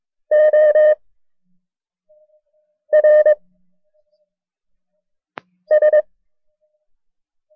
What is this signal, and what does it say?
Morse code, heard as a keyed tone, about 22 wpm, 610 Hz. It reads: O R S
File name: FreeFiles / CW-practice / KB6MT-tape1-side2